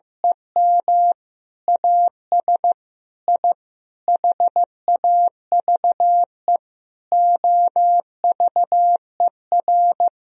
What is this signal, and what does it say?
Morse code, heard as a keyed tone, about 16 wpm, 700 Hz. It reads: EM AS I HAVE OVER
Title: Sentences from Top 1000 Words - Review Part 1 - 15wpm